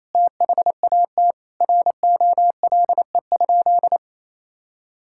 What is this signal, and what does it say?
Morse code, heard as a keyed tone, about 28 wpm, 700 Hz. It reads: THAT ROLE?
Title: Sentences from Top 1000 Words - Review Part 6 - 28wpm